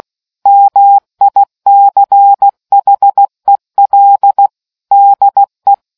Morse code, heard as a keyed tone, about 16 wpm, 800 Hz. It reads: MICHEL DE